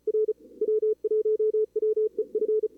Morse code, heard as a keyed tone, about 35 wpm, 440 Hz. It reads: R W1WEF
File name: CWSprint / Feb2022 / k5tr-feb-na-sprint-cw-k5ot -2300